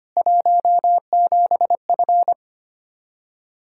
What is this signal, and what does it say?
Morse code, heard as a keyed tone, about 25 wpm, 700 Hz. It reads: 17F
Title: Practice QSO - Exchange 2 - Info - 25wpm